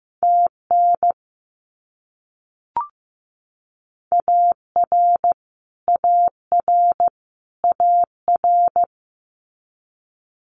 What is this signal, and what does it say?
Morse code, heard as a keyed tone, about 15 wpm, 700 Hz. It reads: TN AR AR AR
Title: US States Repeated 3x 15wpm